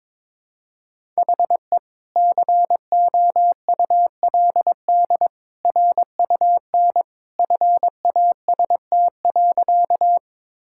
Morse code, heard as a keyed tone, about 21 wpm, 700 Hz. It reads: HE COULD RUN FAST.
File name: practice / 22wpm / Sentences with 2-Word Phrases - 22wpm